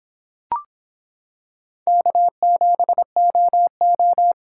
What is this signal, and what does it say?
Morse code, heard as a keyed tone, about 26 wpm, 700 Hz. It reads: K7OO